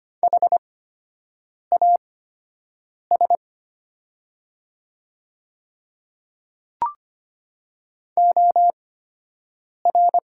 Morse code, heard as a keyed tone, about 25 wpm, spaced as for 7 wpm, 700 Hz. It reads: HAS OR
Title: 8x Character Spacing - Top 100 Words - 25wpm